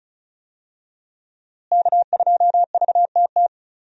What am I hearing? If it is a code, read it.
Morse code, heard as a keyed tone, about 34 wpm, 700 Hz. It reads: K2VTT
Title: USA Call Signs - 5 Characters - Repeated 2x - 35wpm